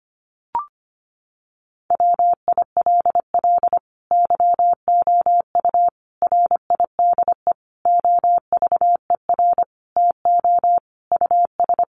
Morse code, heard as a keyed tone, about 25 wpm, 700 Hz. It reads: WILL YOU RIDE OVER TO US
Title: Sentences from Top 1200 Words 25wpm